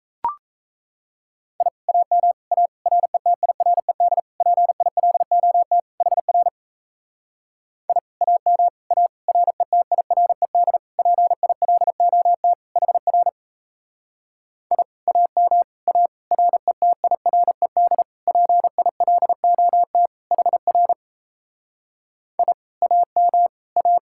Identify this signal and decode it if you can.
Morse code, heard as a keyed tone, about 37 wpm, 700 Hz. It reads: I AM A RETIRED PILOT HR I AM A RETIRED PILOT HR I AM A RETIRED PILOT HR I AM A